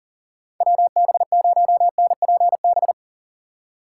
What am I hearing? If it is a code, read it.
Morse code, heard as a keyed tone, about 40 wpm, 700 Hz. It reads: WB0NPB
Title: USA Call Signs - 4x Character Spacing - 40wpm